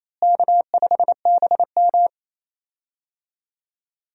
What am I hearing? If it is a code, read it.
Morse code, heard as a keyed tone, about 27 wpm, 700 Hz. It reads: K5BM
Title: USA Call Signs - 4 Characters - Repeated 3x - 28wpm